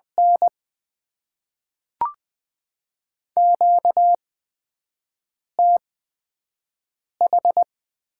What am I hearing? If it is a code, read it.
Morse code, heard as a keyed tone, about 20 wpm, spaced as for 5 wpm, 700 Hz. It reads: N QTH
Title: Top 90 QSO Elements - 8x Character Spacing - 20wpm